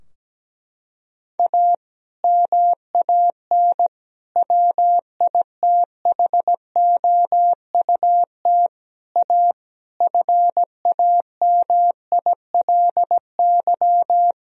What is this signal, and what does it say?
Morse code, heard as a keyed tone, about 17 wpm, 700 Hz. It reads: A MAN WITHOUT A FAMILY
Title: Sentences from Top 1000 Words - Review Part 3 - 17wpm